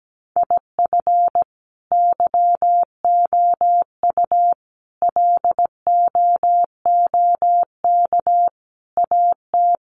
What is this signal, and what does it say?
Morse code, heard as a keyed tone, about 17 wpm, 700 Hz. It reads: IF YOU LOOK AT
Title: Sets of 5 Words 17wpm